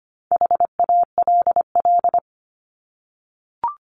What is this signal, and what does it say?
Morse code, heard as a keyed tone, about 25 wpm, 700 Hz. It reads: HALL
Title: Top 1751-2000 Words 25wpm